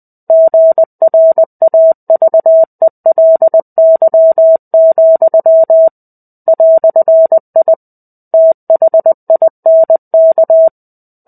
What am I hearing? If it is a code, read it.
Morse code, heard as a keyed tone, about 20 wpm, 650 Hz. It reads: GRAVELY, "I THINK
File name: 05-Chapter-05 20 WPM